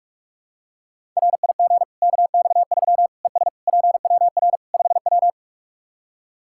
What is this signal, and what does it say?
Morse code, heard as a keyed tone, about 44 wpm, 700 Hz. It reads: RIG KX3 ES PWR 5W
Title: Practice QSO - Exchange 2 - Info - 45wpm